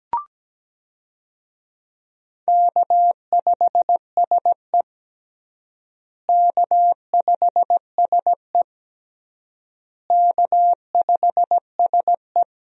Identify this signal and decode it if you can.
Morse code, heard as a keyed tone, about 17 wpm, 700 Hz. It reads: K5SE K5SE K5SE
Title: USA Call Signs - 4 Characters - Repeated 3x - 17wpm